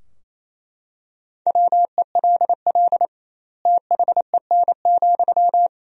Morse code, heard as a keyed tone, about 27 wpm, 700 Hz. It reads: WELL THEN,
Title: 2x Word-Spacing - Sentences from Top 200 Words - 28wpm